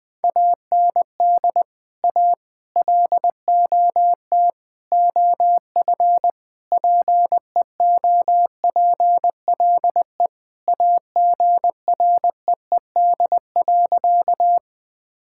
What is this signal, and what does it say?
Morse code, heard as a keyed tone, about 20 wpm, 700 Hz. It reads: AND A LOT OF PEOPLE AGREED.
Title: Sentences with 5-Word Phrases - 20wpm